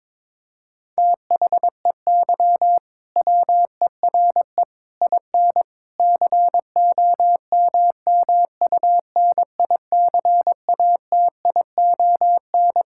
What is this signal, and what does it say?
Morse code, heard as a keyed tone, about 22 wpm, 700 Hz. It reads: THEY WERE IN COMMUNICATION
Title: Sentences from Top 1000 Words - Review Part 6 - 22wpm